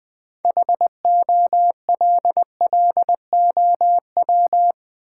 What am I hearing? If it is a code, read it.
Morse code, heard as a keyed tone, about 20 wpm, 700 Hz. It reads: HOLLOW
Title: 6 Letter Words - First 3 Letters Given - 20wpm